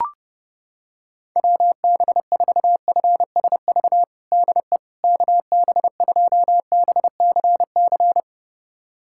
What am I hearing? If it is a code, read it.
Morse code, heard as a keyed tone, about 30 wpm, 700 Hz. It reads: WB4FSV DE KB2BCC
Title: USA Callsign DE Callsign - 6 Characters - 30wpm